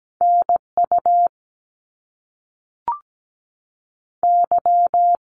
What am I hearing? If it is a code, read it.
Morse code, heard as a keyed tone, about 17 wpm, spaced as for 5 wpm, 700 Hz. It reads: NUY
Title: Canadian Provinces and Territories 17wpm